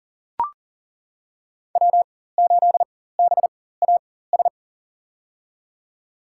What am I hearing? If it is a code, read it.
Morse code, heard as a keyed tone, about 40 wpm, spaced as for 19 wpm, 700 Hz. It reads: W8BAS
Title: USA Call Signs - 5 Characters - 4x Character Spacing - 40wpm